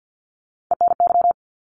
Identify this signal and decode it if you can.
Morse code, heard as a keyed tone, about 50 wpm, 700 Hz. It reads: ENY
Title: ARRL Sections 50wpm